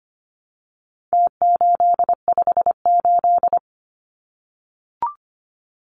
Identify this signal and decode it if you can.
Morse code, heard as a keyed tone, about 25 wpm, 700 Hz. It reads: T858
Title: Numbers - Contest Style - 4-Digit 25wpm